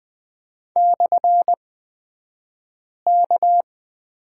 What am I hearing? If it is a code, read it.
Morse code, heard as a keyed tone, about 20 wpm, spaced as for 5 wpm, 700 Hz. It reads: /K
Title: 249 Lesson 40d - BK - Two Characters 20wpm